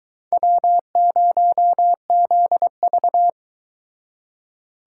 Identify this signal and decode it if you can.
Morse code, heard as a keyed tone, about 24 wpm, 700 Hz. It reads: W0ZV